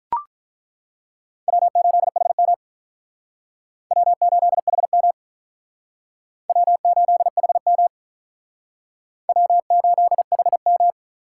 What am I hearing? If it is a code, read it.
Morse code, heard as a keyed tone, about 48 wpm, 700 Hz. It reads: W8HM W8HM W8HM W8HM